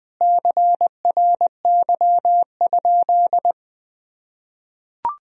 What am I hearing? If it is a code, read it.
Morse code, heard as a keyed tone, about 20 wpm, 700 Hz. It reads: CRY?
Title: Sentences from Top 1000 Words - Review Part 2 - 20wpm